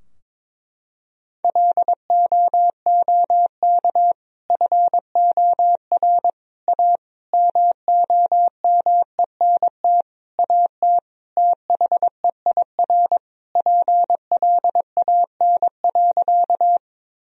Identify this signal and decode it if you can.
Morse code, heard as a keyed tone, about 22 wpm, 700 Hz. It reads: LOOK FOR A MOMENT AT THEIR PLAN.
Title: Sentences from Top 1000 Words - Review Part 2 - 22wpm